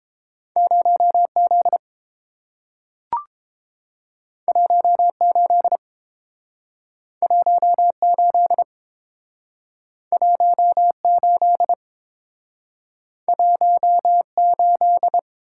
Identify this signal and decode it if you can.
Morse code, heard as a keyed tone, about 30 wpm, 700 Hz. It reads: 0Z 18 18 18 18